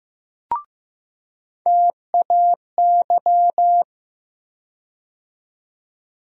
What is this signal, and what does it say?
Morse code, heard as a keyed tone, about 15 wpm, 700 Hz. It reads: TAY
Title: USA Call Sign Postfixes - 3 Letters 15wpm